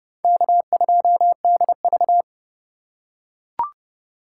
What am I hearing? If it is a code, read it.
Morse code, heard as a keyed tone, about 31 wpm, 700 Hz. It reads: K2DV